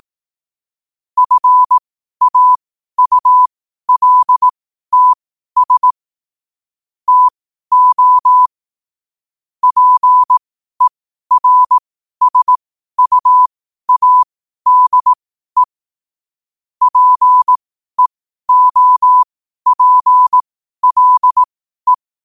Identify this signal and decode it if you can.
Morse code, heard as a keyed tone, about 18 wpm, spaced as for 12 wpm, 1000 Hz. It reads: FAULTS TO PERSUADE PEOPLE
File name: qotdfiles / 12wpm